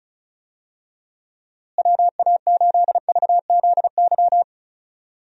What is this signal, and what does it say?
Morse code, heard as a keyed tone, about 36 wpm, 700 Hz. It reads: WA8VZY
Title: USA Call Signs - 6 Characters - Repeated 3x - 35wpm